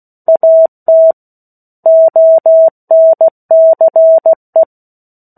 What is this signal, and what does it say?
Morse code, heard as a keyed tone, about 16 wpm, 650 Hz. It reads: AT ONCE
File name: 03-Chapter-03 16 WPM